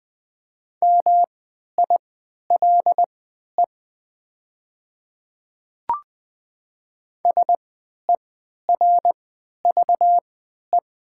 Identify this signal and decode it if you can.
Morse code, heard as a keyed tone, about 20 wpm, spaced as for 11 wpm, 690 Hz. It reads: MILE SERVE
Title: 3x Character Spacing - Top 500 Words - 20wpm